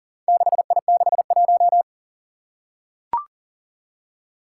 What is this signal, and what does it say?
Morse code, heard as a keyed tone, about 39 wpm, 700 Hz. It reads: 6I61
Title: Four Challenging Letter-Number Combinations 40wpm